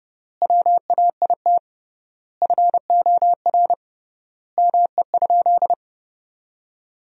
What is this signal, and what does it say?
Morse code, heard as a keyed tone, about 30 wpm, 700 Hz. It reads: WAIT FOR ME?
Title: 3x Word-Spacing - Sentences from Top 500 Words - 30wpm